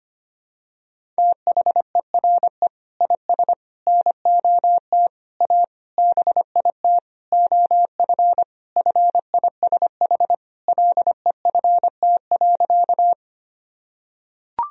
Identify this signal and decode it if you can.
Morse code, heard as a keyed tone, about 25 wpm, 700 Hz. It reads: THERE IS NOT A BIT OF FISH LEFT.
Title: Sentences from Top 1000 Words - Review Part 1 - 25wpm